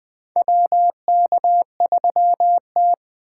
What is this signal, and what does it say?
Morse code, heard as a keyed tone, about 20 wpm, 700 Hz. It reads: WK3T
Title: USA Call Signs - 4 Characters - 2x Character Spacing - 20wpm